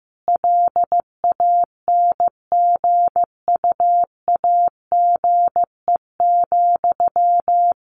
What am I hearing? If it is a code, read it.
Morse code, heard as a keyed tone, about 15 wpm, 700 Hz. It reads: LANGUAGE,
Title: Sentences from Top 600 Words 15wpm